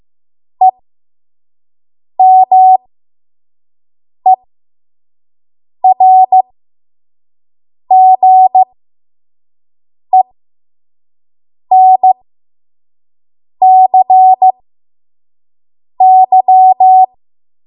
Morse code, heard as a keyed tone, about 15 wpm, spaced as for 5 wpm, 750 Hz. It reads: EMERGENCY
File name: Morse / 05WPM